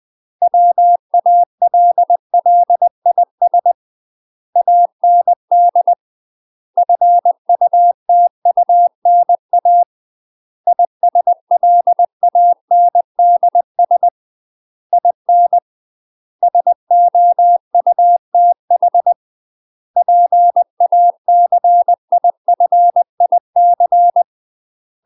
Morse code, heard as a keyed tone, about 20 wpm, 700 Hz. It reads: WALLIS AND FUTUNA ISLANDS IN SOUTH PACIFIC